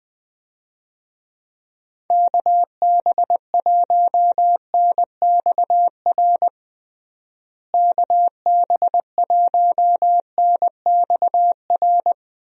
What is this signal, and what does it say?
Morse code, heard as a keyed tone, about 20 wpm, 700 Hz. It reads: KB1NXR KB1NXR
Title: USA Call Signs - 6 Characters - Repeated 3x - 20wpm